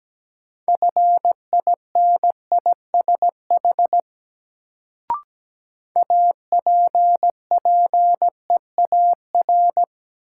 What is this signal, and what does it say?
Morse code, heard as a keyed tone, about 17 wpm, 700 Hz. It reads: FINISH APPEAR
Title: Sentences from Top 1200 Words - New Word Drill Alt - 17wpm